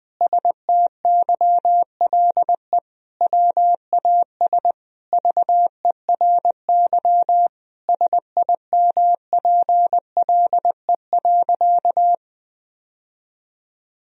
Morse code, heard as a keyed tone, about 20 wpm, 700 Hz. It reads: STYLE WAS VERY SIMPLE.
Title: Sentences from Top 1000 Words - Review Part 4 - 20wpm